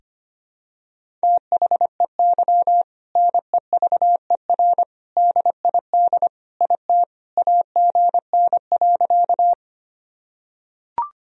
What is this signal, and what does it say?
Morse code, heard as a keyed tone, about 25 wpm, 700 Hz. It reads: THEY NEVER DID IT AGN.
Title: Sentences from Top 300 Words and Abbreviations 25wpm